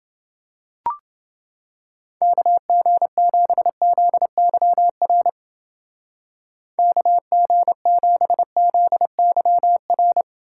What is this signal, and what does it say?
Morse code, heard as a keyed tone, about 29 wpm, 700 Hz. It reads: KG7ZYR KG7ZYR